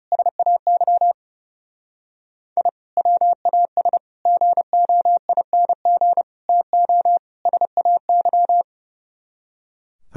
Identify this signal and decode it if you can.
Morse code, heard as a keyed tone, about 36 wpm, 700 Hz. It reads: SAY I WAS GOING TO SAY
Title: Speed-Racing - Sets of 5 Words 30wpm